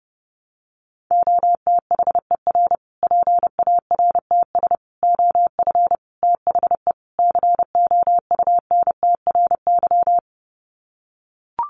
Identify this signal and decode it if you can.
Morse code, heard as a keyed tone, about 30 wpm, 700 Hz. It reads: OTHER PARTS OF THE COUNTRY
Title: Sets of 5 Words Encore 30wpm